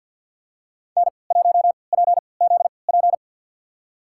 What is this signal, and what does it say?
Morse code, heard as a keyed tone, about 53 wpm, spaced as for 29 wpm, 700 Hz. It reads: N1PZP
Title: USA Call Signs - 5 Characters - 3x Character Spacing - 50wpm